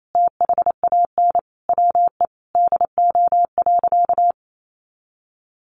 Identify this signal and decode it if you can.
Morse code, heard as a keyed tone, about 28 wpm, 700 Hz. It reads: THAN WE DO.
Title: Sentences from Top 100 Words 28wpm